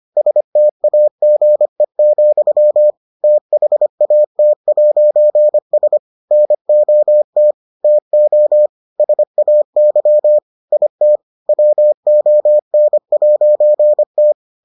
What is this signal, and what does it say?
Morse code, heard as a keyed tone, about 25 wpm, 600 Hz. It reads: STAGE, THAT'S NOT TO SAY IT WON'T